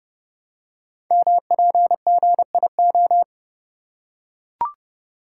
Morse code, heard as a keyed tone, about 30 wpm, 700 Hz. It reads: MPGIO
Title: Code Groups - 5 Letters - 30wpm